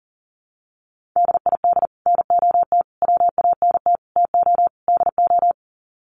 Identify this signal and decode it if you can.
Morse code, heard as a keyed tone, about 40 wpm, 700 Hz. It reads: DID NOT WANT TO DO
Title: Sets of 5 Words Encore 40wpm